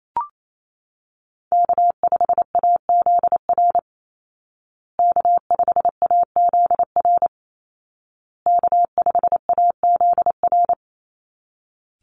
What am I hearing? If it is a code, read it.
Morse code, heard as a keyed tone, about 28 wpm, 700 Hz. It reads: K5AZR K5AZR K5AZR
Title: Call Signs Repeated 2x - 28wpm